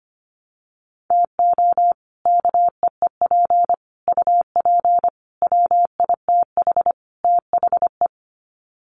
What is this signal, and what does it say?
Morse code, heard as a keyed tone, about 25 wpm, 700 Hz. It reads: TO KEEP UP WITH THE
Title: Sets of 5 Words Encore 25wpm